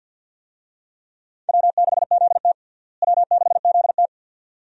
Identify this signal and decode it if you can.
Morse code, heard as a keyed tone, about 48 wpm, 700 Hz. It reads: W6ZT W6ZT
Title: USA Call Signs - 4 Characters - Repeated 2x - 50wpm